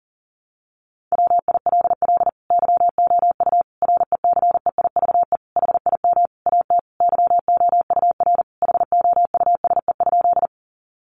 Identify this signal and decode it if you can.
Morse code, heard as a keyed tone, about 40 wpm, 700 Hz. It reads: WILL YOU RECEIVE HIM AT YOUR HOUSE?
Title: Sentences from Top 1000 Words 40wpm